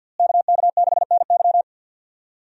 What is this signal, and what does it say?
Morse code, heard as a keyed tone, about 49 wpm, 700 Hz. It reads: KK6NY
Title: USA Call Signs - 5 Characters - Repeated 2x - 50wpm